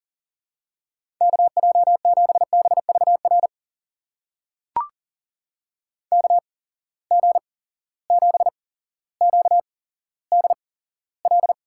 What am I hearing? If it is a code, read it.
Morse code, heard as a keyed tone, about 39 wpm, spaced as for 17 wpm, 700 Hz. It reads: KJ7BVR KG7QDL
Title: USA Call Signs - 6 Characters - 8x Character Spacing - 40wpm